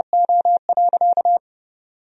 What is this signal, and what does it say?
Morse code, heard as a keyed tone, about 29 wpm, 700 Hz. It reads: O.
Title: Sentences from Top 1000 Words - Review Part 3 - 30wpm